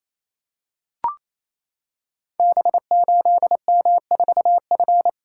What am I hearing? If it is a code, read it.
Morse code, heard as a keyed tone, about 27 wpm, 700 Hz. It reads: B8M4F
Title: Code Groups - 5 Letter:Numbers - 28wpm